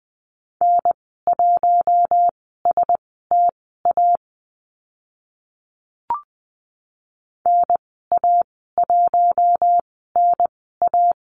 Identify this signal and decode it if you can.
Morse code, heard as a keyed tone, about 20 wpm, spaced as for 14 wpm, 700 Hz. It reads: N1STA NA1NA
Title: 042 Lesson 8g - Number 1 - Call Signs 20wpm